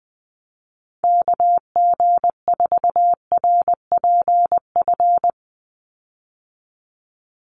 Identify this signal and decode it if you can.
Morse code, heard as a keyed tone, about 20 wpm, 700 Hz. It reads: KG4RPF
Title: USA Call Signs 20wpm